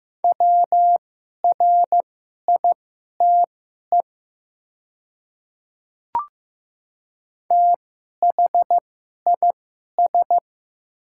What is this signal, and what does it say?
Morse code, heard as a keyed tone, about 15 wpm, spaced as for 11 wpm, 700 Hz. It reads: WRITE THIS
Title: 2x Character Spacing - Top 100 Words - 15wpm